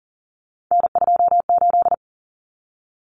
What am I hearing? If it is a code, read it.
Morse code, heard as a keyed tone, about 40 wpm, 700 Hz. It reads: N28
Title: Common Cut Numbers - 3-Digit 40wpm